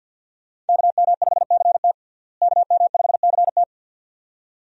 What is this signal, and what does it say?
Morse code, heard as a keyed tone, about 50 wpm, 700 Hz. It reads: KM5CT KM5CT
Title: USA Call Signs - 5 Characters - Repeated 2x - 50wpm